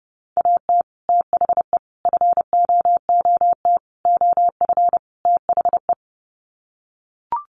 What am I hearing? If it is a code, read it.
Morse code, heard as a keyed tone, about 28 wpm, 700 Hz. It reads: AT THE FOOT OF THE
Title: Sets of 5 Words 30wpm